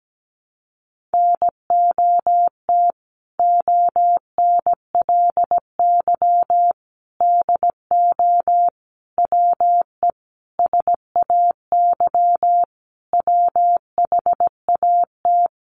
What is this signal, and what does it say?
Morse code, heard as a keyed tone, about 17 wpm, 700 Hz. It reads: NOT ONLY DO WE SAY WHAT
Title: Sentences from Top 1200 Words 17wpm